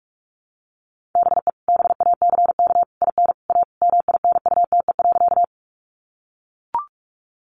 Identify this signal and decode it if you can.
Morse code, heard as a keyed tone, about 44 wpm, 700 Hz. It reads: BE BACK IN A MINUTE.
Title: Sentences from Top 500 Words 45wpm